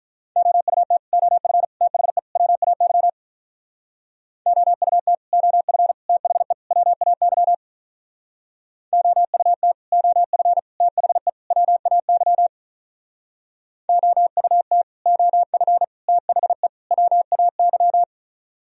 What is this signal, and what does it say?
Morse code, heard as a keyed tone, about 44 wpm, 700 Hz. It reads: OUT OF THE WAY OUT OF THE WAY OUT OF THE WAY OUT OF THE WAY